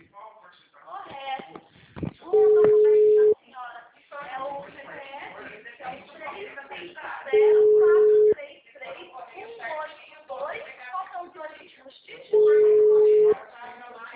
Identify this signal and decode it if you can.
Ringback tone (Europe, 425 Hz), with speech.
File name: out- -2002- - - .85398